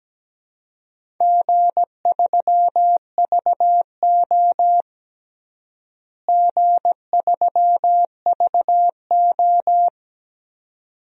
Morse code, heard as a keyed tone, about 17 wpm, 700 Hz. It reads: G3VO G3VO
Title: DXCC-100 Call Signs - 4 Characters - Repeated 2x - 17wpm